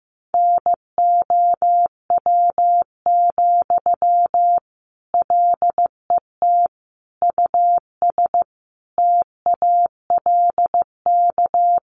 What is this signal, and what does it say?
Morse code, heard as a keyed tone, about 15 wpm, 700 Hz. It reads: NOW, LET US TALK
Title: Sentences from Top 800 Words 15wpm